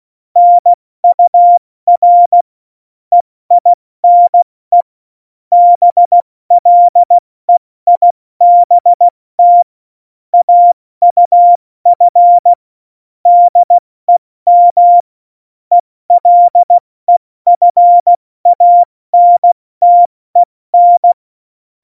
Morse code, heard as a keyed tone, about 16 wpm, 700 Hz. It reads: NUR EINE BLEIBT AUF DEM ELEFANTEN
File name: Morsekurs / Witze / Witz 03.Juli 201, 70 F80